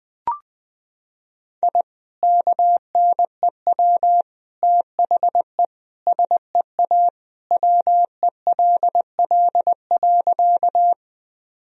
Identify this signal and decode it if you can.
Morse code, heard as a keyed tone, about 20 wpm, 700 Hz. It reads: I KNEW THE SEA WELL.
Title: Sentences from Top 1000 Words - Review Part 1 - 20wpm